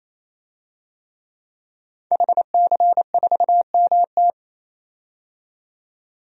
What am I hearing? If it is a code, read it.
Morse code, heard as a keyed tone, about 28 wpm, 700 Hz. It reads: HC4MT
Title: DXCC-100 Call Signs - 5 Characters - 28wpm